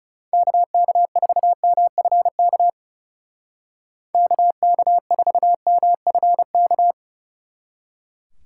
Morse code, heard as a keyed tone, about 36 wpm, 700 Hz. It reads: KK4MFK KK4MFK
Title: Speed Racing - USA 2x3 Call Signs 30wpm